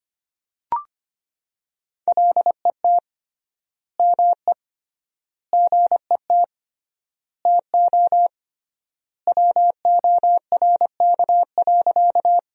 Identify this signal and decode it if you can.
Morse code, heard as a keyed tone, about 25 wpm, 700 Hz. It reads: LET ME GET TO WORK.
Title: 3x Word-Spacing - Sentences from Top 300 Words - 25wpm